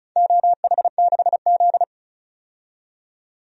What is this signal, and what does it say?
Morse code, heard as a keyed tone, about 36 wpm, 700 Hz. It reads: OH6Z
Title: DXCC-100 Call Signs - 4 Characters - 35wpm